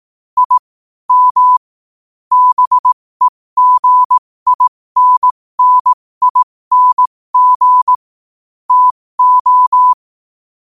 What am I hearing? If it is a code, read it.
Morse code, heard as a keyed tone, about 18 wpm, spaced as for 15 wpm, 1000 Hz. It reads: I M BEGINNING TO
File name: 15wpm